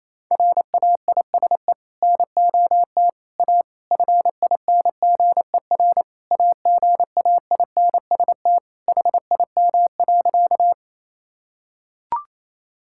Morse code, heard as a keyed tone, about 28 wpm, 700 Hz. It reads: RAISE NOT A FINGER AGAINST HIM.
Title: Sentences from Top 1600 Words 28wpm